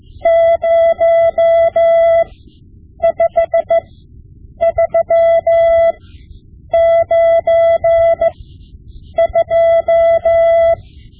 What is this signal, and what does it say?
Morse code, heard as a keyed tone, about 15 wpm, spaced as for 8 wpm, 670 Hz. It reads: 05392
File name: Morse Lessons / Morse Lessons / Tape-5B